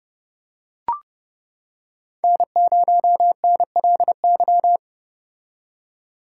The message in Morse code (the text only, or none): N0NLY